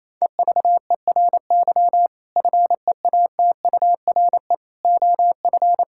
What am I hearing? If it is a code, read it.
Morse code, heard as a keyed tone, about 28 wpm, 700 Hz. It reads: EVERY FEATURE OF
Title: Sentences from Top 1000 Words - Review Part 5 - 28wpm